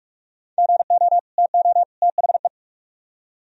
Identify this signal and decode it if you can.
Morse code, heard as a keyed tone, about 43 wpm, 700 Hz. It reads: GO TO THE